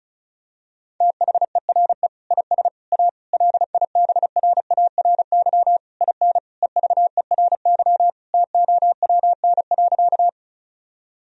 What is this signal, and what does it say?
Morse code, heard as a keyed tone, about 36 wpm, 700 Hz. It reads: THERE IS A LIBRARY IN EVERY TOWN.
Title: Sentences from Top 1000 Words - Review Part 4 - 35wpm